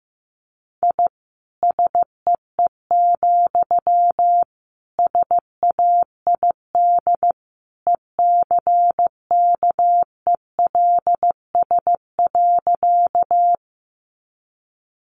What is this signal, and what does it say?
Morse code, heard as a keyed tone, about 15 wpm, 700 Hz. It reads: I SEE, SAID ECKELS.
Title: A Sound Of Thunder 15wpm